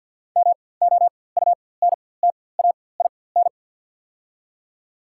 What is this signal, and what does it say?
Morse code, heard as a keyed tone, about 53 wpm, spaced as for 24 wpm, 700 Hz. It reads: MOUNTAIN
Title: 4x Character Spacing - Top 500 Words - 50wpm